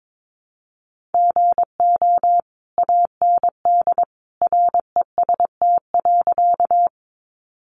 Morse code, heard as a keyed tone, about 22 wpm, 700 Hz. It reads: GO AND REST.
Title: Sentences from Top 500 Words 22wpm